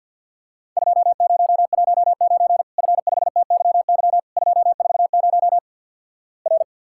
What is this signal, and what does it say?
Morse code, heard as a keyed tone, about 50 wpm, 700 Hz. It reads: 2019 F5TYY 240 R
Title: Worked All Europe DX Contest - QTC Exchange with Repeats - 50wpm